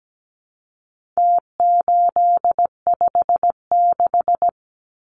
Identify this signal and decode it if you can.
Morse code, heard as a keyed tone, about 17 wpm, 700 Hz. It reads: T856
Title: Numbers - Contest Style - 4-Digit 17wpm